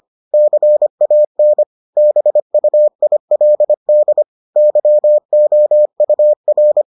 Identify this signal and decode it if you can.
Morse code, heard as a keyed tone, about 25 wpm, 600 Hz. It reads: CAN BUILD YOUR